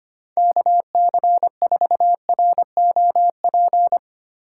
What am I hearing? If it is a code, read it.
Morse code, heard as a keyed tone, about 25 wpm, 700 Hz. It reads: KC4ROP
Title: USA Call Signs - 2x Character Spacing - 25wpm